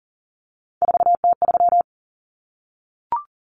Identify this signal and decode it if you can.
Morse code, heard as a keyed tone, about 40 wpm, 700 Hz. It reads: 4T3
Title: Common Cut Numbers - 3-Digit 40wpm